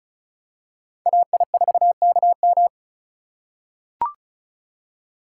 Morse code, heard as a keyed tone, about 34 wpm, 700 Hz. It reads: AI4KM
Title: USA Call Signs - 5 Characters - 3x Character Spacing - 35wpm